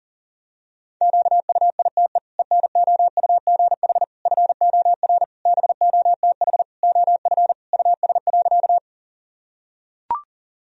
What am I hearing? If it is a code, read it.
Morse code, heard as a keyed tone, about 40 wpm, 700 Hz. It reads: QUITE ENOUGH FOR BOTH OF US.
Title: Sentences from Top 1000 Words - Review Part 1 - 40wpm